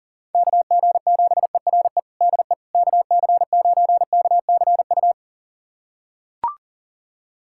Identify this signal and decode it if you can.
Morse code, heard as a keyed tone, about 40 wpm, 700 Hz. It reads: KG7ERE DE KC9KCU
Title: USA Callsign DE Callsign 40wpm